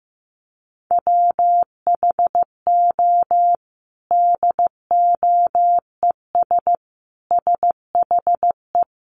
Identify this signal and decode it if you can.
Morse code, heard as a keyed tone, about 15 wpm, 700 Hz. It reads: WHO DOES SHE
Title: Sentences from Top 1300 Words 15wpm